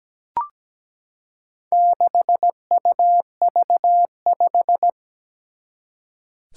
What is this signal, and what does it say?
Morse code, heard as a keyed tone, about 17 wpm, 700 Hz. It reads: 6UV5
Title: Four Challenging Letter-Number Combinations 17wpm